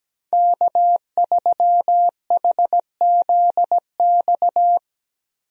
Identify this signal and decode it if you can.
Morse code, heard as a keyed tone, about 17 wpm, 700 Hz. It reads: K3HZX
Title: USA Call Signs - 5 Characters - Repeated 2x - 17wpm